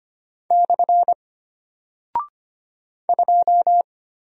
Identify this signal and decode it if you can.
Morse code, heard as a keyed tone, about 25 wpm, 700 Hz. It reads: / 2
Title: Single Character (Letter:Number:Punctuation:Prosign) Alt 25wpm